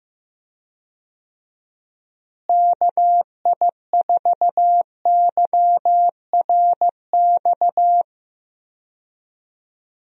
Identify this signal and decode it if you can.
Morse code, heard as a keyed tone, about 15 wpm, 700 Hz. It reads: KI4YRX
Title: USA Call Signs - 6 Characters - Repeated 2x - 15wpm